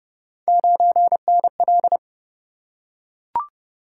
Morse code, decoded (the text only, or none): 9NL